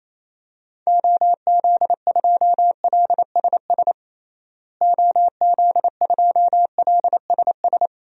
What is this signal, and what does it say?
Morse code, heard as a keyed tone, about 28 wpm, 700 Hz. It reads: OZ2LSS OZ2LSS
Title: DXCC-100 Call Signs - 6 Characters - Repeated 3x - 28wpm